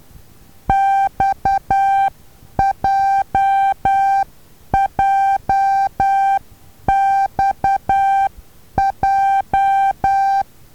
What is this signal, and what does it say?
Morse code, heard as a keyed tone, about 10 wpm, 780 Hz. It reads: XJJXJ
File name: Taviro HA6VA B